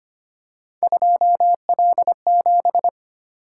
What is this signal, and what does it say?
Morse code, heard as a keyed tone, about 25 wpm, 700 Hz. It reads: 2L7
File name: Three Letter-Number Combinations - 25wpm